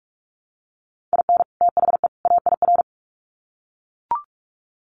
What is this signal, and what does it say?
Morse code, heard as a keyed tone, about 44 wpm, 700 Hz. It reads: IN THE AIR
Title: Sets of 3 Words 45wpm